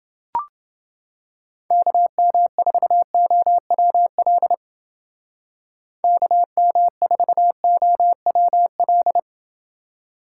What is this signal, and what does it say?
Morse code, heard as a keyed tone, about 29 wpm, 700 Hz. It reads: KM4OWL KM4OWL